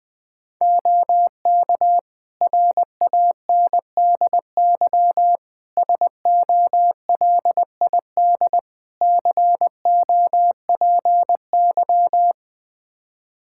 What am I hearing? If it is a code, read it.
Morse code, heard as a keyed tone, about 21 wpm, 700 Hz. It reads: OK RANDY SOLID COPY
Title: Practice QSO - Exchange 3 - More Info - 20wpm